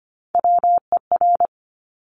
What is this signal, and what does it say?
Morse code, heard as a keyed tone, about 25 wpm, 700 Hz. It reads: WER
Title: Top Three Letter Combinations in Words 25wpm